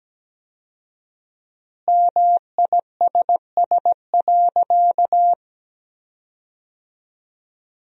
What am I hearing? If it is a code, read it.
Morse code, heard as a keyed tone, about 17 wpm, 700 Hz. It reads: MISS.
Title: 4x Word-Spacing - Sentences from Top 500 Words - 17wpm